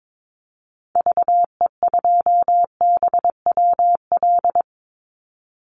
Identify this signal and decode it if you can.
Morse code, heard as a keyed tone, about 22 wpm, 700 Hz. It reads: VE2BWL
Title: Call Signs Repeated 2x - 22wpm